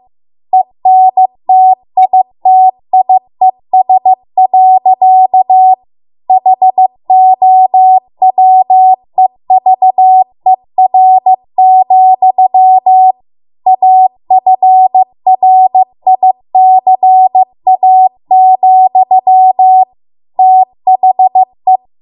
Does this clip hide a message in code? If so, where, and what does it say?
Yes: Morse code from 0.5 to 21.8 s, heard as a keyed tone, about 15 wpm, 750 Hz. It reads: ENTITIES. HOWEVER, AFRICA, THE